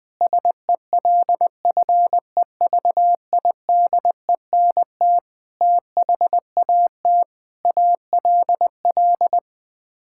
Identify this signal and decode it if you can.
Morse code, heard as a keyed tone, about 20 wpm, 700 Hz. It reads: SELFEVIDENT THAT ALL
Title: Sets of 9 Words - 20wpm